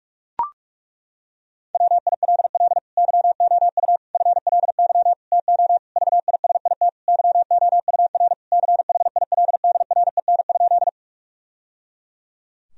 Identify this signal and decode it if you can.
Morse code, heard as a keyed tone, about 41 wpm, 700 Hz. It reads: WILL YOU FLY TO VISIT YOUR CHILDREN?
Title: Sentences from Top 1000 Words - Review Part 2 - 45wpm